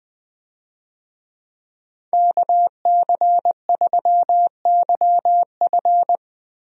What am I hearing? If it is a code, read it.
Morse code, heard as a keyed tone, about 20 wpm, 700 Hz. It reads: KC3YF
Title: USA Call Signs - 5 Characters - 20wpm